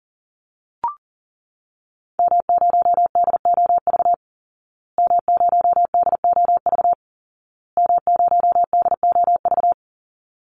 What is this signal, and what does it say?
Morse code, heard as a keyed tone, about 40 wpm, 700 Hz. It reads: M0DOV M0DOV M0DOV
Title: Call Signs Repeated 2x - 40wpm